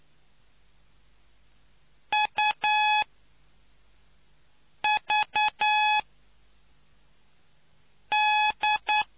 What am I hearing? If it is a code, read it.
Morse code, heard as a keyed tone, about 9 wpm, spaced as for 4 wpm, 850 Hz. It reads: UVD